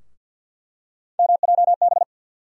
Morse code, heard as a keyed tone, about 48 wpm, 700 Hz. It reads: MJB
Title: USA Call Sign Postfixes - 3 Letters 50wpm